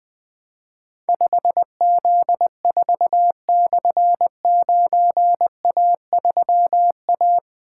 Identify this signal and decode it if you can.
Morse code, heard as a keyed tone, about 20 wpm, 700 Hz. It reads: 5Z4/9A3A
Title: Difficult Call Signs 20wpm